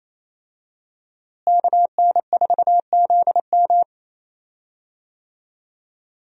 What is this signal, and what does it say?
Morse code, heard as a keyed tone, about 28 wpm, 700 Hz. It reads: KN4ZM
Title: USA Call Signs - 5 Characters - Repeated 2x - 28wpm